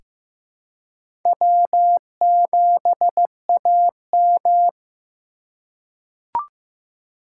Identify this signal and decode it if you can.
Morse code, heard as a keyed tone, about 15 wpm, 700 Hz. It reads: W7AM
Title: USA Call Signs - 4 Characters - 4x Character Spacing - 15wpm